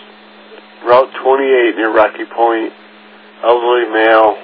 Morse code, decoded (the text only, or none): none